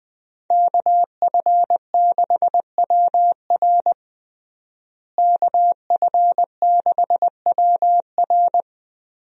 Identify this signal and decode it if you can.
Morse code, heard as a keyed tone, about 20 wpm, 700 Hz. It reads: KF6WR KF6WR
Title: USA Call Signs - 5 Characters - Repeated 2x - 20wpm